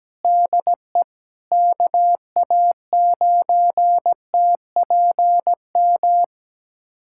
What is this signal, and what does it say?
Morse code, heard as a keyed tone, about 17 wpm, 700 Hz. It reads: DE KA9TPM
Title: USA Callsign DE Callsign - 6 Characters - 17wpm